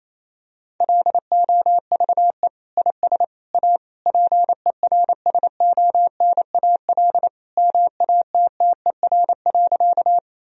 Morse code, heard as a keyed tone, about 28 wpm, 700 Hz. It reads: LOVE IS A PERSONAL MATTER.
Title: Sentences from Top 1000 Words - Review Part 4 - 28wpm